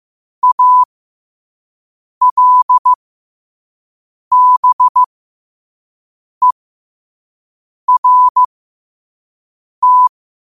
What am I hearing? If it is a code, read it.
Morse code, heard as a keyed tone, about 15 wpm, spaced as for 5 wpm, 1000 Hz. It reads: ALBERT
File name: qotdfiles / 5wpm